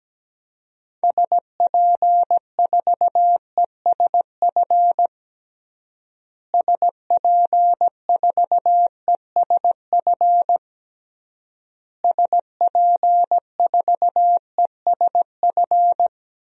Morse code, heard as a keyed tone, about 17 wpm, 700 Hz. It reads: SP4ESF SP4ESF SP4ESF
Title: DXCC-100 Call Signs - 6 Characters - Repeated 3x - 17wpm